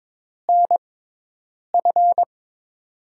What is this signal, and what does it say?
Morse code, heard as a keyed tone, about 22 wpm, spaced as for 8 wpm, 700 Hz. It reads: NF
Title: USA Call Signs - 5 Characters - 6x Character Spacing - 22wpm